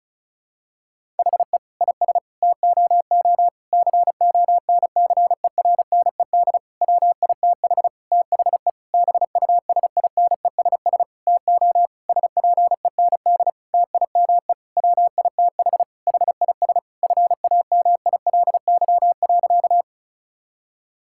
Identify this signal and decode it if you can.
Morse code, heard as a keyed tone, about 36 wpm, 700 Hz. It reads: HE IS TOO CONCERNED WITH THE BUSINESS TO SPEND TIME WITH HIS FAMILY.
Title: Sentences from Top 1000 Words - Review Part 6 - 35wpm